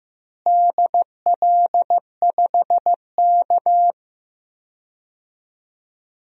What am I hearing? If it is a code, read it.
Morse code, heard as a keyed tone, about 15 wpm, 700 Hz. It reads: DL5K
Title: DXCC-100 Call Signs - 4 Characters - Repeated 2x - 15wpm